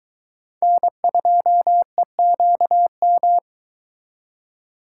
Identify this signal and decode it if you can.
Morse code, heard as a keyed tone, about 22 wpm, 700 Hz. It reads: N2EQM